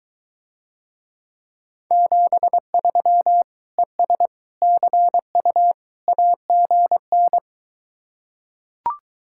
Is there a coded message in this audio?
Yes: Morse code from 1.9 to 7.4 s, heard as a keyed tone, about 23 wpm, 700 Hz. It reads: 73 ES CU AGN